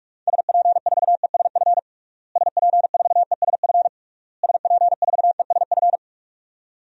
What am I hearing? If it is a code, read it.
Morse code, heard as a keyed tone, about 41 wpm, 700 Hz. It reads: SP4ESF SP4ESF SP4ESF
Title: DXCC-100 Call Signs - 6 Characters - Repeated 3x - 45wpm